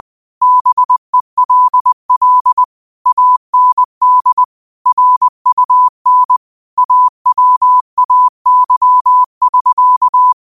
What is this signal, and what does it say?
Morse code, heard as a keyed tone, about 20 wpm, 1000 Hz. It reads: BELL AND RUN AWAY <SK>